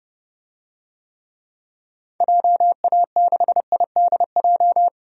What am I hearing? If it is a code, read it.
Morse code, heard as a keyed tone, about 30 wpm, 700 Hz. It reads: JA6IDJ
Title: DXCC-100 Call Signs - 6 Characters - 30wpm